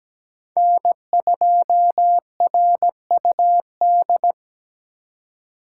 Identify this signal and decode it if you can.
Morse code, heard as a keyed tone, about 17 wpm, 700 Hz. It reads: N2RUD
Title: USA Call Signs - 5 Characters - 2x Character Spacing - 17wpm